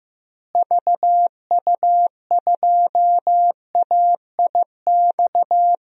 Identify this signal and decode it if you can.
Morse code, heard as a keyed tone, about 15 wpm, 700 Hz. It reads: VU2AIX
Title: DXCC-100 Call Signs - 6 Characters - Repeated 2x - 15wpm